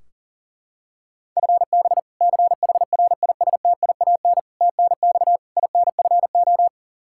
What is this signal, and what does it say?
Morse code, heard as a keyed tone, about 40 wpm, 700 Hz. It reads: FB CHRISTIAN TNX INFO
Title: K1USN SST QSO Practice 40wpm